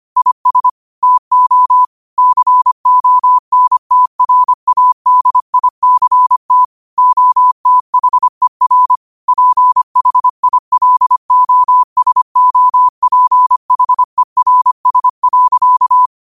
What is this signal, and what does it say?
Morse code, heard as a keyed tone, about 25 wpm, 1000 Hz. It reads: IS TO CONTRADICT OTHER PHILOSOPHERS.
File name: qotdfiles / 25wpm